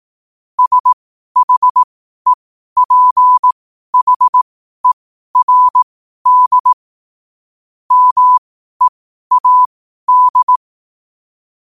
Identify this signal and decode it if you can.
Morse code, heard as a keyed tone, about 18 wpm, spaced as for 12 wpm, 1000 Hz. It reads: SHEPHERD MEAD
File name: qotdfiles / 12wpm